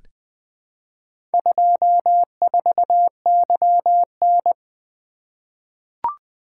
Morse code, heard as a keyed tone, about 20 wpm, 700 Hz. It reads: 24YN
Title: Four Letter-Number Combinations - 20wpm